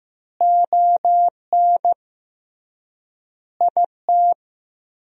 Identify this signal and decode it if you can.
Morse code, heard as a keyed tone, about 16 wpm, 700 Hz. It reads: ON IT
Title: 3x Word-Spacing - Sentences from Top 400 Words - 15wpm